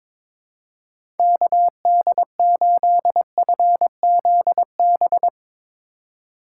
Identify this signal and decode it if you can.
Morse code, heard as a keyed tone, about 22 wpm, 700 Hz. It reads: KD8FZB
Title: USA Call Signs - 6 Characters - 4x Character Spacing - 22wpm